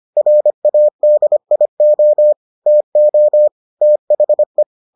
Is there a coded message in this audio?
Yes: Morse code from 0.2 to 4.6 s, heard as a keyed tone, about 25 wpm, 600 Hz. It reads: RADIO TO THE